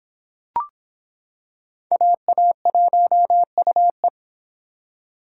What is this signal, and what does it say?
Morse code, heard as a keyed tone, about 26 wpm, 700 Hz. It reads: AA1UE